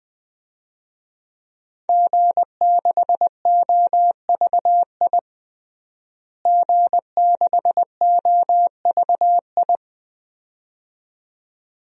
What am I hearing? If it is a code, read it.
Morse code, heard as a keyed tone, about 20 wpm, 700 Hz. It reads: G6OVI G6OVI
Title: DXCC-100 Call Signs - 5 Characters - Repeated 2x - 20wpm